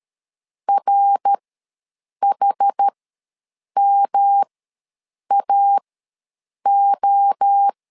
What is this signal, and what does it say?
Morse code, heard as a keyed tone, about 13 wpm, spaced as for 7 wpm, 800 Hz. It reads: RHMAO
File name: LEZIONI / 3 MWRO / 3 MWRO ws 12wpm 1914